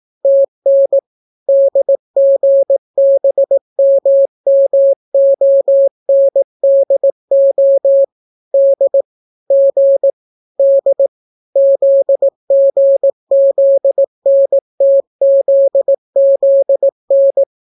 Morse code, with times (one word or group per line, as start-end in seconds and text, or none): 0.2-1.0 TN
1.5-8.0 DGBMMONDO
8.5-9.0 D
9.5-10.1 G
10.6-11.1 D
11.6-17.4 ZGZNTZZN